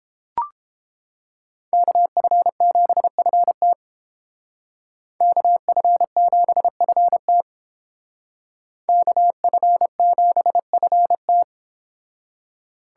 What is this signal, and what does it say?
Morse code, heard as a keyed tone, about 29 wpm, 700 Hz. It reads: KF7FT KF7FT KF7FT